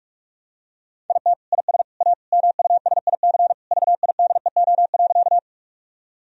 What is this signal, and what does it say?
Morse code, heard as a keyed tone, about 46 wpm, 700 Hz. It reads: IT IS A MUSIC VIDEO.
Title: Sentences from Top 1000 Words - Review Part 4 - 45wpm